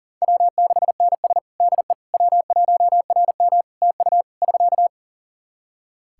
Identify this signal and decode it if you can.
Morse code, heard as a keyed tone, about 40 wpm, 700 Hz. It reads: W6NS DE W1RM TU <SK>
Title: K1USN SST QSO Practice 40wpm